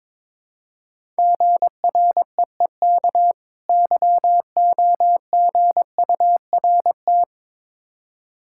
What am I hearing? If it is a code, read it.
Morse code, heard as a keyed tone, about 22 wpm, 690 Hz. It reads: GREEK YOGURT
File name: Difficult Words with Hints - 22wpm